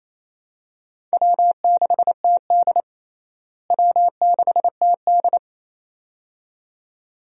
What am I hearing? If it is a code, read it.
Morse code, heard as a keyed tone, about 27 wpm, 700 Hz. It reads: W6TD W6TD
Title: USA Call Signs - 4 Characters - Repeated 2x - 28wpm